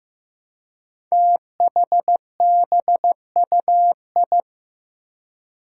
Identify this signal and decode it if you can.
Morse code, heard as a keyed tone, about 15 wpm, 700 Hz. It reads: THBUI
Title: Five Challenging Letter-Number Combinations 15wpm